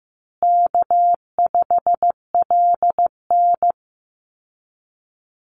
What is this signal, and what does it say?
Morse code, heard as a keyed tone, about 15 wpm, 700 Hz. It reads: K5LN
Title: Easy Call Signs Repeated 3x - 15wpm